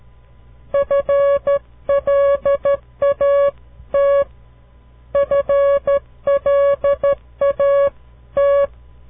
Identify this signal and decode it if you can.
Morse code, heard as a keyed tone, about 13 wpm, 560 Hz. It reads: FLAT FLAT
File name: CW Learning Programs / farnsworth / unit 2